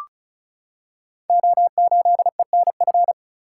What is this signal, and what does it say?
Morse code, heard as a keyed tone, about 33 wpm, 700 Hz. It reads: O8ENF
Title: Code Groups - 5 Letter:Numbers - 35wpm